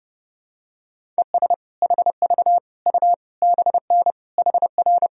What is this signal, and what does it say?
Morse code, heard as a keyed tone, about 30 wpm, 700 Hz. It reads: ES HV U BN HR
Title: Sentences from Top 200 Words and Abbreviations 30wpm